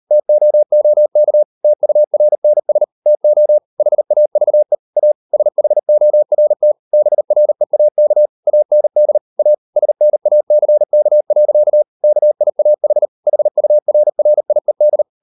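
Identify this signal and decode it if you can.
Morse code, heard as a keyed tone, about 38 wpm, 600 Hz. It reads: TOOK TURNS TO HAVE A SHORT BREAK AND A SNACK. KIAH HURRIED
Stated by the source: Morse code practice at 38 WPM, sending Ken Follett's novel Never in English, from CW med Gnister.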